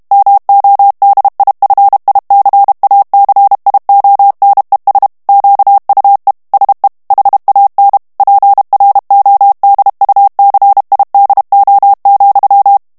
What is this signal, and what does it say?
Morse code, heard as a keyed tone, about 32 wpm, 800 Hz. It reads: MODIFICACIONES QUE SE HAN PRODUCIDO,
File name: ESPAGNOL TEXTE EN CLAIR 2 VITESSE 30 MOTS MINUTES